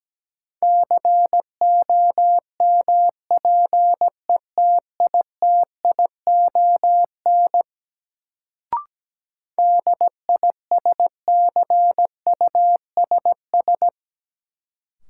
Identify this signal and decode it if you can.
Morse code, heard as a keyed tone, about 17 wpm, 700 Hz. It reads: COMPETITION DISCUSS
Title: Sentences from Top 1000 Words - New Word Drill - 17wpm